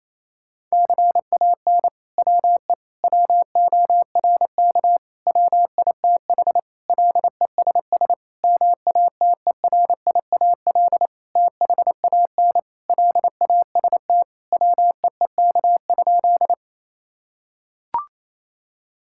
Morse code, heard as a keyed tone, about 28 wpm, 700 Hz. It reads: CAN WE WORK WITH LESS MATERIAL THAN LAST WEEK?
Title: Sentences from Top 1000 Words - Review Part 5 - 28wpm